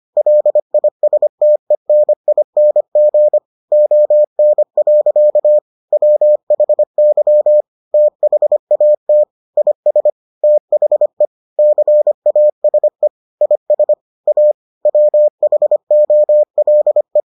Morse code, heard as a keyed tone, about 25 wpm, 600 Hz. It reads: LISTENING ON. WHY THAT IS THE CASE IS A WHOLE